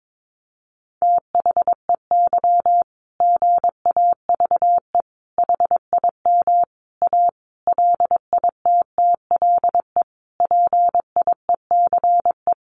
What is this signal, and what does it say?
Morse code, heard as a keyed tone, about 22 wpm, 700 Hz. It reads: THEY GAVE HIM A LITTLE PIECE
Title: Sentences from Top 1500 Words 22wpm